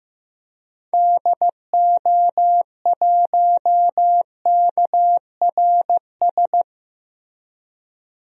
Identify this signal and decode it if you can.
Morse code, heard as a keyed tone, about 15 wpm, 700 Hz. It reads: DO1KRS
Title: DXCC-100 Call Signs - 6 Characters - Repeated 3x - 15wpm